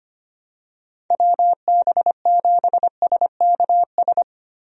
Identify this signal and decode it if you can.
Morse code, heard as a keyed tone, about 25 wpm, 700 Hz. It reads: WB7SKS
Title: USA Call Signs - 6 Characters - Repeated 3x - 25wpm